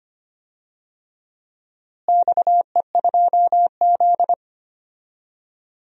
Morse code, heard as a keyed tone, about 25 wpm, 700 Hz. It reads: XE2Z
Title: DXCC-100 Call Signs - 4 Characters - Repeated 2x - 25wpm